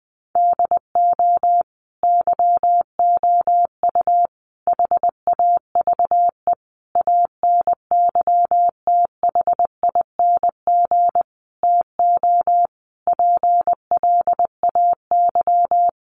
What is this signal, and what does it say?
Morse code, heard as a keyed tone, about 20 wpm, 700 Hz. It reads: DO YOU HAVE ANYTHING TO PLAY
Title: Sentences from Top 1000 Words 20wpm